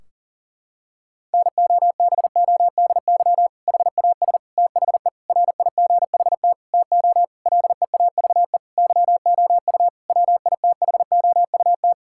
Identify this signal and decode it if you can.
Morse code, heard as a keyed tone, about 40 wpm, 700 Hz. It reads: NOBODY HAS THE RIGHT TO LEAVE YOU WITHOUT
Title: Sentences from Top 1600 Words 40wpm